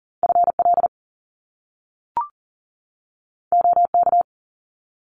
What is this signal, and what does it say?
Morse code, heard as a keyed tone, about 42 wpm, 700 Hz. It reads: FL OK
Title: US States 40wpm